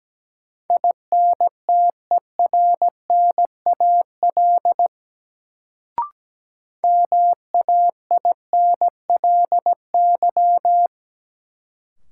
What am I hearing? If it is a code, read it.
Morse code, heard as a keyed tone, about 17 wpm, 700 Hz. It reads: INTERNAL MAINLY
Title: Sentences from Top 1400 Words - New Word Drill Alt - 17wpm